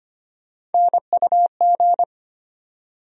Morse code, heard as a keyed tone, about 25 wpm, 700 Hz. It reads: NUG
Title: USA Call Sign Postfixes - 3 Letters 25wpm